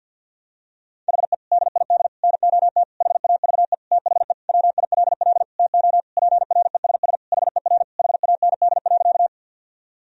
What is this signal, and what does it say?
Morse code, heard as a keyed tone, about 52 wpm, 700 Hz. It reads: HE DID NOT HAVE THE WILL TO PRESS HER HAND.
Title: Sentences from Top 1000 Words - Review Part 1 - 50wpm